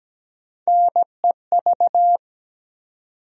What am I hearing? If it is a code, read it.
Morse code, heard as a keyed tone, about 17 wpm, 700 Hz. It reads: NEV
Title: USA Call Sign Postfixes - 3 Letters 17wpm